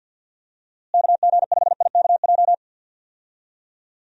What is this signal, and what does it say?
Morse code, heard as a keyed tone, about 51 wpm, 700 Hz. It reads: KG5IKJ
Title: USA Call Signs - 6 Characters - Repeated 2x - 50wpm